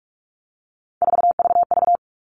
Morse code, heard as a keyed tone, about 44 wpm, 700 Hz. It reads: 4VV
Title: All Cut Numbers - 3-Digit 45wpm